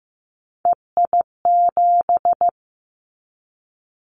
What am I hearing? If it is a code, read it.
Morse code, heard as a keyed tone, about 15 wpm, 700 Hz. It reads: EI7
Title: Callsign Prefixes for Top 50 DXCC - 15wpm